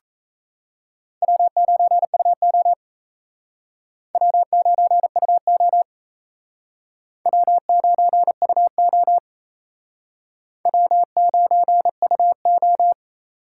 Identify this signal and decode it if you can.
Morse code, heard as a keyed tone, about 36 wpm, 700 Hz. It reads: W9UO W9UO W9UO W9UO